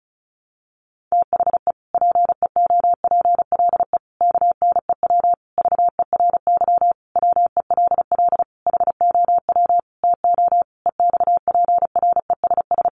Morse code, heard as a keyed tone, about 34 wpm, 700 Hz. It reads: THE PEOPLE KNEW VERY WELL HOW TO EXPRESS
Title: Sentences from Top 1100 Words 35wpm